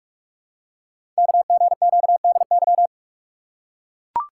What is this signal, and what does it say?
Morse code, heard as a keyed tone, about 44 wpm, 700 Hz. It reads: KGQDY
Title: Code Groups - 5 Letters - 45wpm